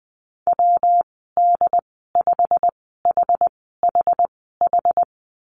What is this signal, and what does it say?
Morse code, heard as a keyed tone, about 20 wpm, spaced as for 14 wpm, 700 Hz. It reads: WD5HHH
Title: 122 Lesson 20g - Number 3 - Call Signs 20wpm